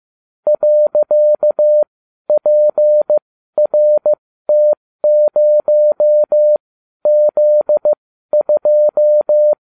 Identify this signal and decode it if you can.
Morse code, heard as a keyed tone, about 15 wpm, spaced as for 12 wpm, 600 Hz. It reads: .PRT0Z2